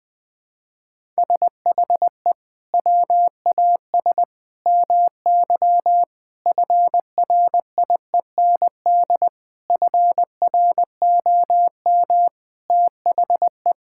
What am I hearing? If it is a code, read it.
Morse code, heard as a keyed tone, about 19 wpm, 700 Hz. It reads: SHE WAS MY FRIEND FROM THE
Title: Sentences from Top 1000 Words - Review Part 5 - 20wpm